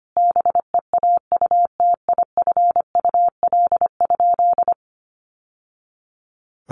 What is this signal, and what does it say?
Morse code, heard as a keyed tone, about 25 wpm, 700 Hz. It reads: BEAUTIFUL?
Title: Sentences from Top 1200 Words 25wpm